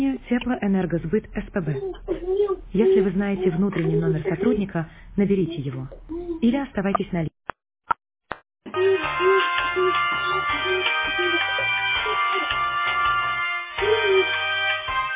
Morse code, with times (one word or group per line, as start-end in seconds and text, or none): none